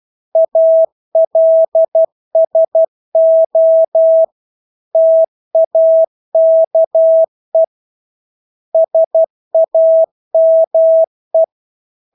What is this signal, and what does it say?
Morse code, heard as a keyed tone, about 12 wpm, 650 Hz. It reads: ALSO TAKE SAME
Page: Engelske ord 12 wpm